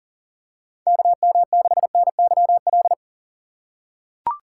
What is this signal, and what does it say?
Morse code, heard as a keyed tone, about 40 wpm, 700 Hz. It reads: KM6NYL
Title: USA Call Signs - 6 Characters - 4x Character Spacing - 40wpm